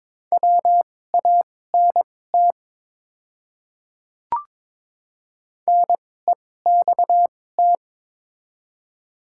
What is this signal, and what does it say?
Morse code, heard as a keyed tone, about 22 wpm, spaced as for 16 wpm, 700 Hz. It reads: WANT NEXT
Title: 2x Character Spacing - Top 300 Words - 22wpm